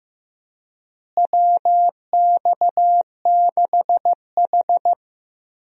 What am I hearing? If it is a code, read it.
Morse code, heard as a keyed tone, about 15 wpm, 700 Hz. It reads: WX6H
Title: USA Call Signs - 4 Characters - Repeated 2x - 15wpm